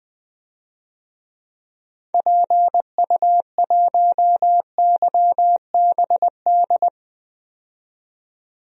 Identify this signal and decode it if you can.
Morse code, heard as a keyed tone, about 20 wpm, 700 Hz. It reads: PU1YBD
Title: DXCC-100 Call Signs - 6 Characters - 20wpm